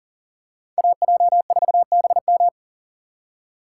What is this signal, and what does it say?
Morse code, heard as a keyed tone, about 40 wpm, 700 Hz. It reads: AJ4BM
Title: USA Call Signs - 5 Characters - 3x Character Spacing - 40wpm